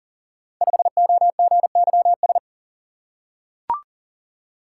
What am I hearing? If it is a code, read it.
Morse code, heard as a keyed tone, about 39 wpm, 700 Hz. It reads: 5OGYS
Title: Code Groups - 5 Letter:Numbers - 40wpm